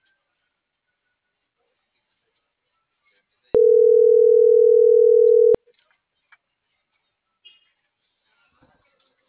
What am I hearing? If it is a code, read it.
Ringback tone (North America), 440 + 480 Hz.